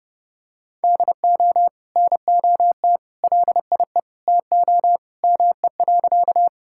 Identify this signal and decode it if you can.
Morse code, heard as a keyed tone, about 30 wpm, 700 Hz. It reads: DO NOT LIE TO ME.
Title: Sentences from Top 1600 Words 30wpm